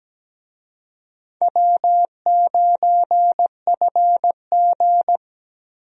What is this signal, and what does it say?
Morse code, heard as a keyed tone, about 17 wpm, 700 Hz. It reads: W9FG
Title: USA Call Signs - 4 Characters - Repeated 2x - 17wpm